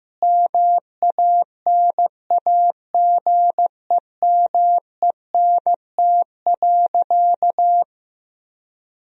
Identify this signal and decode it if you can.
Morse code, heard as a keyed tone, about 15 wpm, 700 Hz. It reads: MANAGEMENT.
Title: Sentences from Top 1000 Words - Review Part 4 - 15wpm